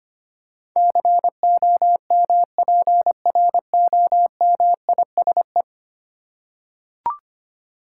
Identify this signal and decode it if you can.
Morse code, heard as a keyed tone, about 25 wpm, 700 Hz. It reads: COMPROMISE
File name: 10 Letter Words - First 3 Letters Given - 25wpm